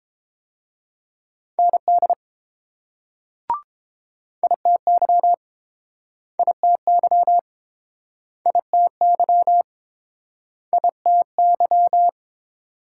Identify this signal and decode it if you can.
Morse code, heard as a keyed tone, about 31 wpm, 700 Hz. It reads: ND ITY ITY ITY ITY